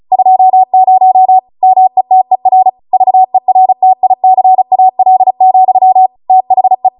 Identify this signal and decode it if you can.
Morse code, heard as a keyed tone, about 36 wpm, 750 Hz. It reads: 20 METER VERTICAL, THE
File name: Morse / 35WPM